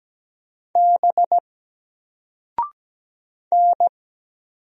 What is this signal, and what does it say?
Morse code, heard as a keyed tone, about 17 wpm, 700 Hz. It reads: B N
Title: Single Letter-Number 17wpm